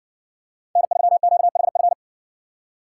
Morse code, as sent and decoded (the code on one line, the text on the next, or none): -. ...-- --.- .... ..-.
N3QHF